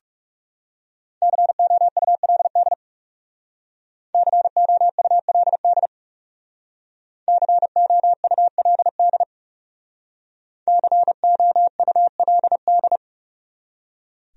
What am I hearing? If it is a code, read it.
Morse code, heard as a keyed tone, about 39 wpm, 700 Hz. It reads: COULD COULD COULD COULD